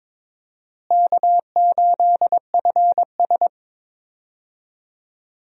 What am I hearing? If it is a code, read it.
Morse code, heard as a keyed tone, about 22 wpm, 700 Hz. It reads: K8FS
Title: USA Call Signs - 4 Characters - Repeated 3x - 22wpm